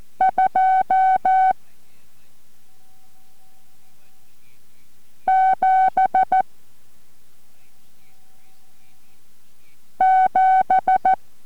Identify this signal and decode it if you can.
Morse code, heard as a keyed tone, about 14 wpm, 760 Hz. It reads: 2 7 7